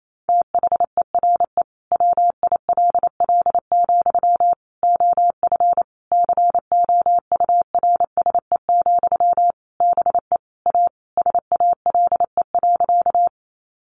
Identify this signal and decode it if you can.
Morse code, heard as a keyed tone, about 28 wpm, 700 Hz. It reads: THERE WILL, OF COURSE, BE A SALE.
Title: Sentences from Top 1400 Words 28wpm